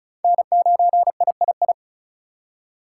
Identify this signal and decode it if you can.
Morse code, heard as a keyed tone, about 34 wpm, 700 Hz. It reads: N9III
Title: USA Call Signs - 5 Characters - 4x Character Spacing - 35wpm